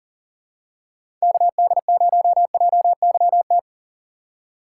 Morse code, heard as a keyed tone, about 40 wpm, 690 Hz. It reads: KD0JYT
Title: USA Call Signs - 6 Characters - 2x Character Spacing - 40wpm